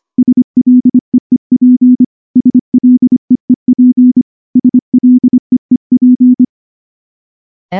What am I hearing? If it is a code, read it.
Morse code, heard as a keyed tone, about 24 wpm, 250 Hz. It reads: SLEEP SLEEP SLEEP
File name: morse / morse-training-25WPM